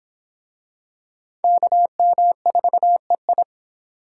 Morse code, heard as a keyed tone, about 25 wpm, 700 Hz. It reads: KM4EI